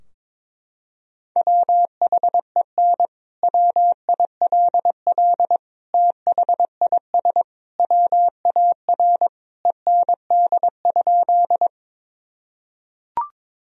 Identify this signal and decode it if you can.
Morse code, heard as a keyed tone, about 22 wpm, 700 Hz. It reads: WHEN WILL THIS WAR END?
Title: Sentences from Top 1000 Words - Review Part 2 - 22wpm